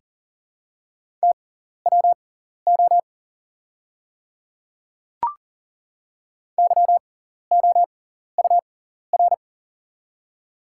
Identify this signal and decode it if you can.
Morse code, heard as a keyed tone, about 40 wpm, spaced as for 14 wpm, 700 Hz. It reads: TWO YOUR
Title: 6x Character Spacing - Top 100 Words - 40wpm